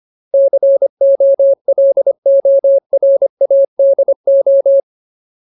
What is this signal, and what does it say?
Morse code, heard as a keyed tone, about 25 wpm, 550 Hz. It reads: COLORADO
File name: Ultimate Word List / 25WPM Ultimate